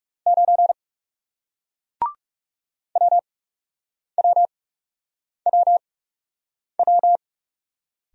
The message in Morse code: 9 WWWW

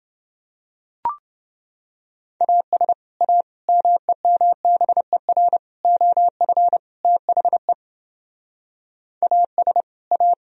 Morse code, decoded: AS A MEMBER OF THE AS A

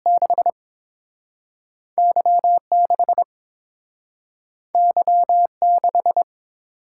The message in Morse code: 6 Y6 Y6

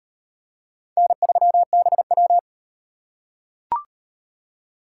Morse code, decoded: N3BW

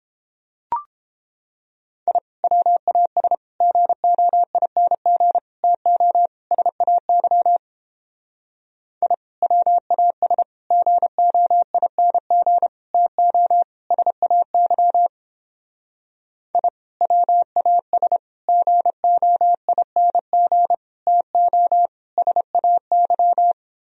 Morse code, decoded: I WAS GOING TO SAY I WAS GOING TO SAY I WAS GOING TO SAY

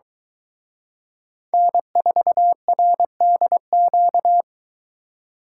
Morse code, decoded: N4RDQ

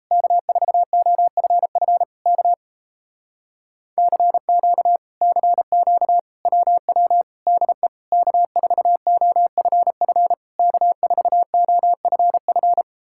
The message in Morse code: K4OFF K CQ CQ WW DE K4OFF K4OFF

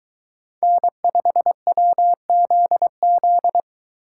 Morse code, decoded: N5WZZ

